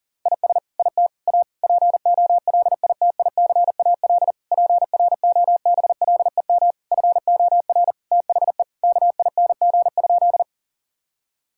Morse code, heard as a keyed tone, about 40 wpm, 700 Hz. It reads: IS IT A POLITICAL PROBLEM FOR THE KING?
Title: Sentences from Top 1000 Words - Review Part 5 - 40wpm